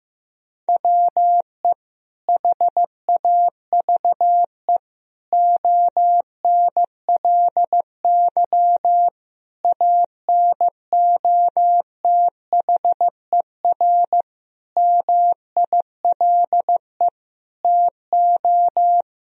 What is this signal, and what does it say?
Morse code, heard as a keyed tone, about 15 wpm, 700 Hz. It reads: WE HAVE ONLY ANOTHER MILE TO
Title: Sentences from Top 1000 Words - Review Part 1 - 15wpm